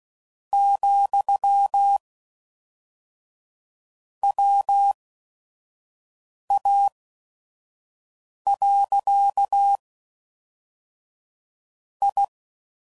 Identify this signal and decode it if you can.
Morse code, heard as a keyed tone, about 16 wpm, spaced as for 5 wpm, 800 Hz. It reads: , WA. I
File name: CW Code Groups @5WPM / QSOs / kc7haa